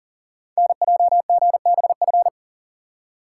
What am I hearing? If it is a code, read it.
Morse code, heard as a keyed tone, about 39 wpm, 700 Hz. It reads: NJGBF
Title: Code Groups - 5 Letters - 40wpm